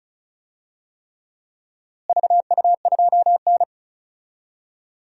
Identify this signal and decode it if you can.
Morse code, heard as a keyed tone, about 36 wpm, 700 Hz. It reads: VU2N
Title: DXCC-100 Call Signs - 4 Characters - Repeated 3x - 35wpm